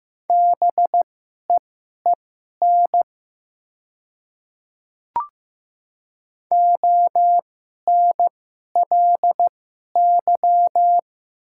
Morse code, heard as a keyed tone, about 15 wpm, spaced as for 11 wpm, 700 Hz. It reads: BEEN ONLY
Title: 2x Character Spacing - Top 200 Words - 15wpm